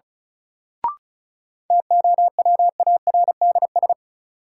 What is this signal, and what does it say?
Morse code, heard as a keyed tone, about 36 wpm, 700 Hz. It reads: TOWARDS
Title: Sentences from Top 1000 Words - Review Part 2 - New Word Drill - 35wpm